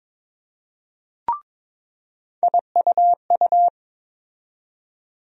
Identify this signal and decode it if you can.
Morse code, heard as a keyed tone, about 22 wpm, 700 Hz. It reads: IUU
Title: USA Call Sign Postfixes - 3 Letters 22wpm